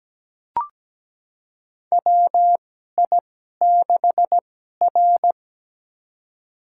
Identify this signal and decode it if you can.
Morse code, heard as a keyed tone, about 17 wpm, spaced as for 12 wpm, 700 Hz. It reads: WI6R
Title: USA Call Signs - 4 Characters - 2x Character Spacing - 17wpm